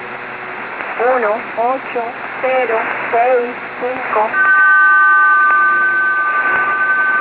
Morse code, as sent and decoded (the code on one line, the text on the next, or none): none